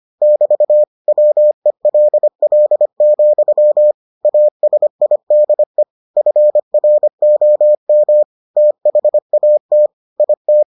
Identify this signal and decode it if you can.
Morse code, heard as a keyed tone, about 25 wpm, 600 Hz. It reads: <BT> WELL, ASIDE FROM THAT IT